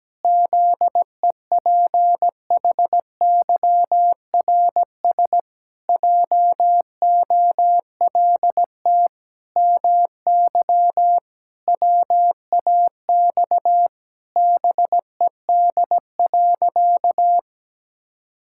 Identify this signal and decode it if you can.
Morse code, heard as a keyed tone, about 17 wpm, 700 Hz. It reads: ZEPHYRS JOLT MY WAX BED.
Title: Pangram Challenge v2 - 17wpm